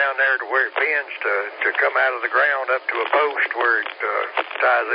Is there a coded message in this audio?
Yes: Emergency Alert System (SAME) tones from 0.7 to 4.0 s, heard as harsh buzzing bursts, with speech.